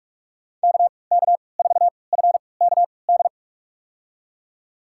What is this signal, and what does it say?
Morse code, heard as a keyed tone, about 46 wpm, spaced as for 25 wpm, 700 Hz. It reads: KK4FKD
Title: USA Call Signs - 6 Characters - 3x Character Spacing - 45wpm